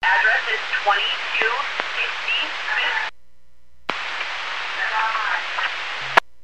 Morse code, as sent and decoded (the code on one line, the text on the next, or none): none